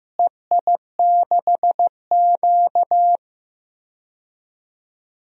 Morse code, heard as a keyed tone, about 15 wpm, 700 Hz. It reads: EI6Q
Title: DXCC-100 Call Signs - 4 Characters - Repeated 3x - 15wpm